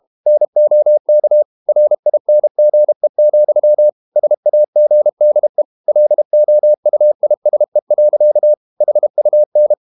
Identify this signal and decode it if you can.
Morse code, heard as a keyed tone, about 32 wpm, 600 Hz. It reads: NOK RINGE, SAGDE LOUISE. HUN
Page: Ferie 32 wpm